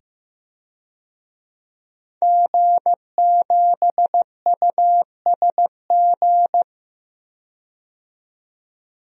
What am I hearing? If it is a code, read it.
Morse code, heard as a keyed tone, about 15 wpm, 700 Hz. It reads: G7USG
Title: DXCC-100 Call Signs - 5 Characters - 15wpm